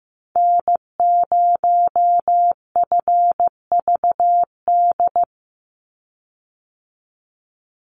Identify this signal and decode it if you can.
Morse code, heard as a keyed tone, about 15 wpm, 700 Hz. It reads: N0FVD
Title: USA Call Signs 15wpm